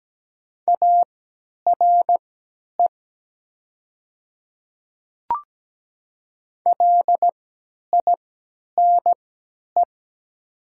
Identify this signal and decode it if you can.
Morse code, heard as a keyed tone, about 17 wpm, spaced as for 10 wpm, 700 Hz. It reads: ARE LINE
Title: 3x Character Spacing - Top 300 Words - 17wpm